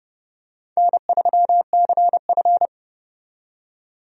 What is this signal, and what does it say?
Morse code, heard as a keyed tone, about 32 wpm, 700 Hz. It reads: N3CF
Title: USA Call Signs - 4 Characters - 30wpm